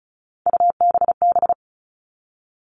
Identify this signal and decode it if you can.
Morse code, heard as a keyed tone, about 34 wpm, 700 Hz. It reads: UBB
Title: All Cut Numbers - 3-Digit 35wpm